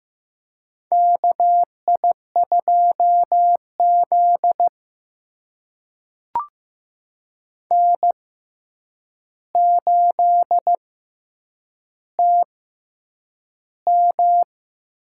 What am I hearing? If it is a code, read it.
Morse code, heard as a keyed tone, about 15 wpm, spaced as for 8 wpm, 700 Hz. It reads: KI2Z N8TM
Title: USA Call Signs - 4 Characters - 6x Character Spacing - 15wpm